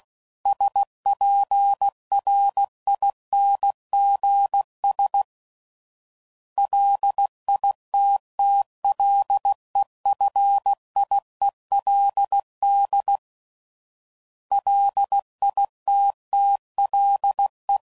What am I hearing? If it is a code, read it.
Morse code, heard as a keyed tone, about 16 wpm, 800 Hz. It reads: SPRINGS LITTLEFIELD LITTLE
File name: CW Code Groups @15WPM / fromSM's / Cities